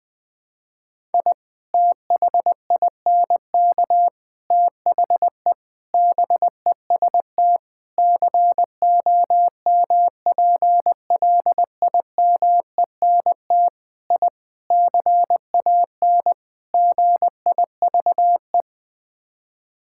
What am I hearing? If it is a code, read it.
Morse code, heard as a keyed tone, about 20 wpm, 700 Hz. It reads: I THINK THE BEST COMPLIMENT I CAN GIVE
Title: Sets of 8 Words - 20wpm